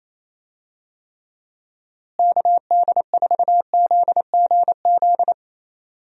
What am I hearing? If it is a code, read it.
Morse code, heard as a keyed tone, about 27 wpm, 700 Hz. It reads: KD4ZGZ
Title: USA Call Signs - 6 Characters - Repeated 3x - 28wpm